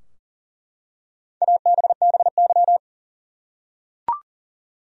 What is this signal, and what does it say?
Morse code, heard as a keyed tone, about 40 wpm, 700 Hz. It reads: ABBY
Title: USA Common Names - 40wpm